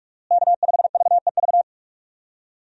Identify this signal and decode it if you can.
Morse code, heard as a keyed tone, about 46 wpm, 700 Hz. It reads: K5VEV
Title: USA Call Signs - 5 Characters - Repeated 3x - 45wpm